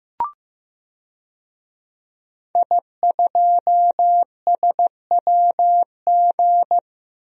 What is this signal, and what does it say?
Morse code, heard as a keyed tone, about 15 wpm, 700 Hz. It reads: I2SWG
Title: DXCC-100 Call Signs - 5 Characters - 15wpm